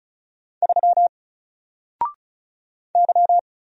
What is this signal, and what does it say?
Morse code, heard as a keyed tone, about 34 wpm, 700 Hz. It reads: 3 Y
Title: Single Letter-Number Alt 35wpm